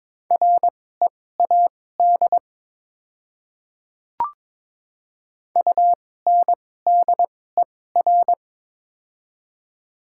Morse code, heard as a keyed tone, about 22 wpm, spaced as for 16 wpm, 700 Hz. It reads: READ UNDER
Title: 2x Character Spacing - Top 200 Words - 22wpm